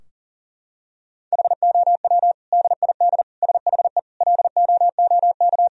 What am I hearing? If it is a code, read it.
Morse code, heard as a keyed tone, about 40 wpm, 700 Hz. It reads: HOW DID SHE LOOK
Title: Sentences from Top 1000 Words - Review Part 3 - 40wpm